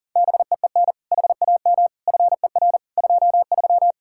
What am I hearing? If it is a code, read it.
Morse code, heard as a keyed tone, about 52 wpm, 700 Hz. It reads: BEEN HAM FER 23